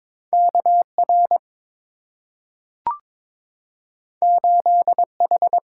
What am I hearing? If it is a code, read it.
Morse code, heard as a keyed tone, about 22 wpm, 690 Hz. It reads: KR 8H
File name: practice / 22wpm / Two Letter-Number Combinations - 22wpm